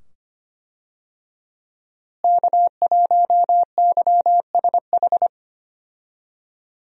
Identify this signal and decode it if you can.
Morse code, heard as a keyed tone, about 25 wpm, 700 Hz. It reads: K1YSH
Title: USA Call Signs - 5 Characters - Repeated 2x - 25wpm